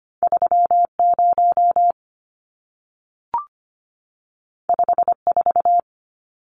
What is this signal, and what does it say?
Morse code, heard as a keyed tone, about 25 wpm, 700 Hz. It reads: 30 54
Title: Numbers - 2-Digit 25wpm